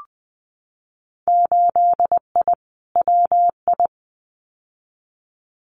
Morse code, heard as a keyed tone, about 20 wpm, 700 Hz. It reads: 8I WI
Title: Winter Field Day Exchanges 20wpm